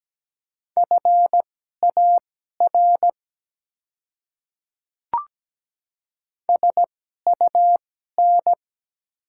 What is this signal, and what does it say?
Morse code, heard as a keyed tone, about 17 wpm, spaced as for 12 wpm, 700 Hz. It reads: FAR SUN
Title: 2x Character Spacing - Top 300 Words - 17wpm